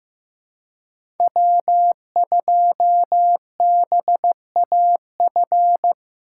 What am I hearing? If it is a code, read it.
Morse code, heard as a keyed tone, about 15 wpm, 700 Hz. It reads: W2BAF
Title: USA Call Signs - 5 Characters - 4x Character Spacing - 15wpm